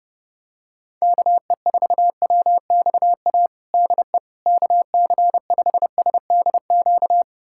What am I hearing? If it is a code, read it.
Morse code, heard as a keyed tone, about 30 wpm, 700 Hz. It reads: KE4WXA DE KC5SDQ
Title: USA Callsign DE Callsign - 6 Characters - 30wpm